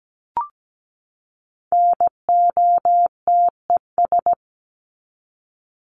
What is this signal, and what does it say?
Morse code, heard as a keyed tone, about 17 wpm, 700 Hz. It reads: NOTES
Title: ES:S Suffix - Top 100 Words with Base Words 17wpm